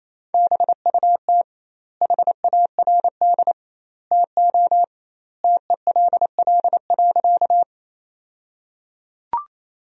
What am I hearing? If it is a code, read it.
Morse code, heard as a keyed tone, about 28 wpm, 700 Hz. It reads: BUT HARD TO TELL.
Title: 2x Word-Spacing - Sentences from Top 300 Words - 28wpm